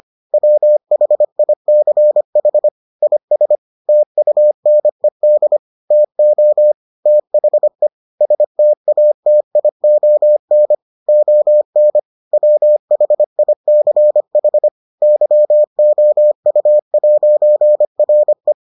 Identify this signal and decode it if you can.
Morse code, heard as a keyed tone, about 25 wpm, 600 Hz. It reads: WHICH IS TUNED TO THE STATION ON WHICH YOU'RE